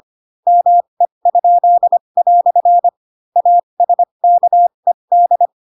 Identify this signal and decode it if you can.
Morse code, heard as a keyed tone, about 25 wpm, spaced as for 22 wpm, 700 Hz. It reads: ME?" ASKED
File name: Pinocchio / Pinocchio ch4 18 25 ik1hge